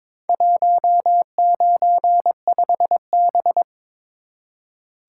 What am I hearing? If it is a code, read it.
Morse code, heard as a keyed tone, about 23 wpm, 700 Hz. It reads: 195B
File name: practice / 22wpm / Four Letter-Number Combinations - 22wpm